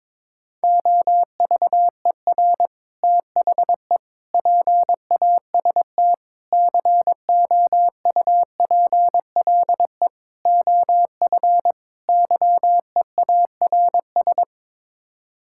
Morse code, heard as a keyed tone, about 21 wpm, 700 Hz. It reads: OVER THE PAST COUPLE OF YEARS
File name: practice / 22wpm / Sets of 6 Words - 22wpm